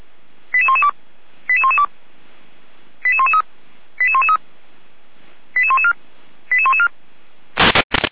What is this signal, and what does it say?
Five-tone selective call (ZVEI), heard as quick runs of five tones, tones of about 70 ms. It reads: 88112 88112 88113 88113 88114 88114
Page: /455/SDR/ZVEI/88118/